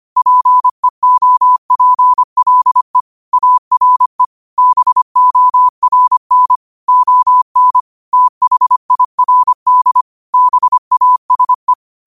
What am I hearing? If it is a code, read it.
Morse code, heard as a keyed tone, about 25 wpm, 1000 Hz. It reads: PEOPLE ARE BORN ON THIRD BASE
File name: qotdfiles / 25wpm